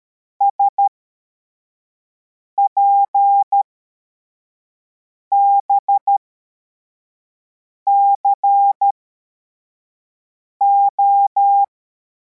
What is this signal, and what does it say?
Morse code, heard as a keyed tone, about 13 wpm, spaced as for 4 wpm, 800 Hz. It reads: SPBCO